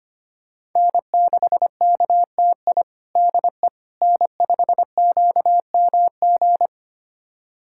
Morse code, heard as a keyed tone, about 25 wpm, 700 Hz. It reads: N6KTI DE N5QMG
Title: USA Callsign DE Callsign - 5 Characters - 25wpm